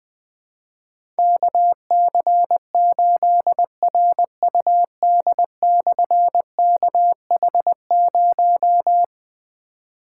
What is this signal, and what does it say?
Morse code, heard as a keyed tone, about 20 wpm, 700 Hz. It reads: KC8RUD/KH0
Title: Difficult Call Signs 20wpm